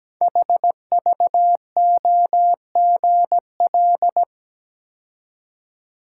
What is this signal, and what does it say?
Morse code, heard as a keyed tone, about 17 wpm, 700 Hz. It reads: HVOGL
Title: Code Groups - 5 Letter:Numbers - 17wpm